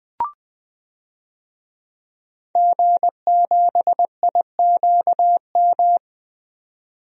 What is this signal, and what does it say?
Morse code, heard as a keyed tone, about 20 wpm, 700 Hz. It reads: G7IQM
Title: DXCC-100 Call Signs - 5 Characters - 20wpm